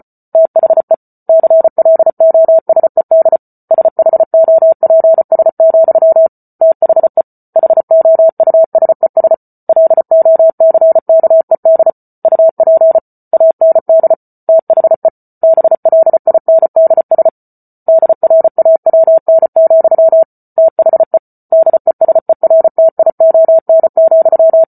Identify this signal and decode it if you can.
Morse code, heard as a keyed tone, about 34 wpm, 650 Hz. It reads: THE CLOSED SHOPS, THE HOUSES LOCKED UP AND THE BLINDS DRAWN, THE DESERTION,